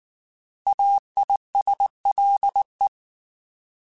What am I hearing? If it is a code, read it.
Morse code, heard as a keyed tone, about 19 wpm, 800 Hz. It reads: AISLE